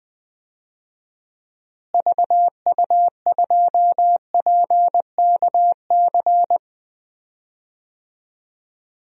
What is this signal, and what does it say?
Morse code, heard as a keyed tone, about 21 wpm, 700 Hz. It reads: VU2PKC
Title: DXCC-100 Call Signs - 6 Characters - Repeated 3x - 20wpm